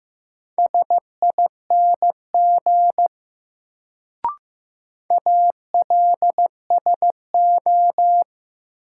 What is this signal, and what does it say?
Morse code, heard as a keyed tone, about 15 wpm, 700 Hz. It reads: SING ALSO
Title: Sentences from Top 400 Words - New Word Drill - 15wpm